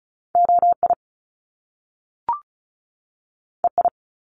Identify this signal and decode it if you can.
Morse code, heard as a keyed tone, about 36 wpm, 700 Hz. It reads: OI EI
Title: Top Two Letter Combinations in Words 35wpm